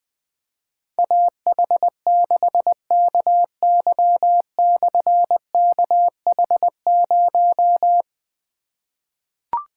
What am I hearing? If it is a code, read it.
Morse code, heard as a keyed tone, about 20 wpm, 700 Hz. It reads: AH6KY/KH0
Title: Difficult Call Signs 20wpm